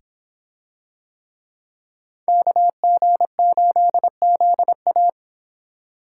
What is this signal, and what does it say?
Morse code, heard as a keyed tone, about 26 wpm, 700 Hz. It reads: KG8ZA